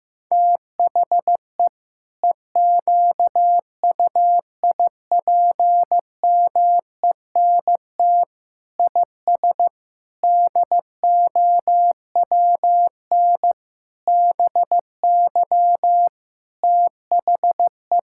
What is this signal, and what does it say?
Morse code, heard as a keyed tone, about 15 wpm, 700 Hz. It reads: THE EQUIPMENT IS DOWN BY THE
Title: Sentences from Top 1000 Words - Review Part 5 - 15wpm